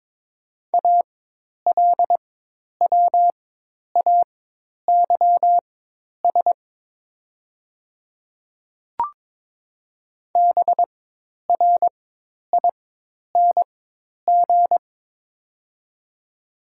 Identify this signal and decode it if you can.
Morse code, heard as a keyed tone, about 23 wpm, spaced as for 10 wpm, 700 Hz. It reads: ALWAYS BRING
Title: 4x Character Spacing - Top 500 Words - 22wpm